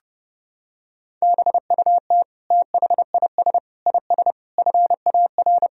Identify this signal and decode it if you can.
Morse code, heard as a keyed tone, about 30 wpm, 700 Hz. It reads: BUT THIS IS FAR
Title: Sentences from Top 1000 Words - Review Part 3 - 30wpm